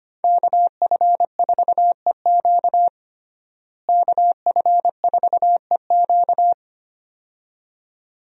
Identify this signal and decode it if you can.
Morse code, heard as a keyed tone, about 25 wpm, 700 Hz. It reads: KF4EQ KF4EQ
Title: USA Call Signs - 5 Characters - Repeated 2x - 25wpm